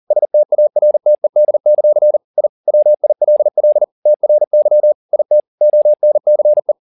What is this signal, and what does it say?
Morse code, heard as a keyed tone, about 40 wpm, 600 Hz. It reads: STARTED; I WILL TRY IT ONCE
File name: the old man and the sea 2 40wpm